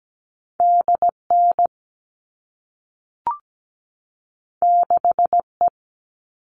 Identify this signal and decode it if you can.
Morse code, heard as a keyed tone, about 17 wpm, 700 Hz. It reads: DN 6E
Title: All Cut Numbers - 2-Digit 17wpm